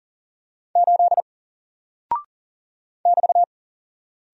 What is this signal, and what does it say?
Morse code, heard as a keyed tone, about 40 wpm, 700 Hz. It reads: 8 <BT>
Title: Single Character (Letter:Number:Punctuation:Prosign) Alt 40wpm